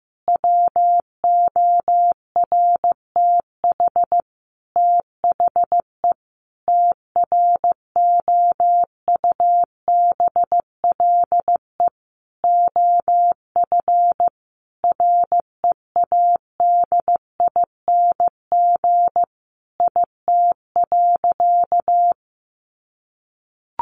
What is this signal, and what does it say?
Morse code, heard as a keyed tone, about 14 wpm, 700 Hz. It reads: WORTH THE TROUBLE OF READING IT.
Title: Sentences from Top 1200 Words 15wpm